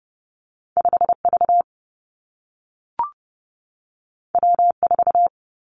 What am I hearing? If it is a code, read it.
Morse code, heard as a keyed tone, about 30 wpm, 700 Hz. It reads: 5 V W 4
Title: Two Challenging Letter-Number Combinations 30wpm